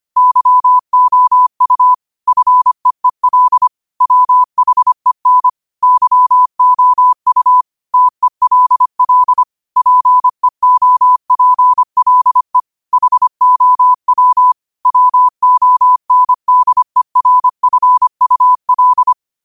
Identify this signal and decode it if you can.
Morse code, heard as a keyed tone, about 25 wpm, 1000 Hz. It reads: YOU FEEL WHEN YOU TELL PEOPLE HOW WONDERFUL
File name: qotdfiles / 25wpm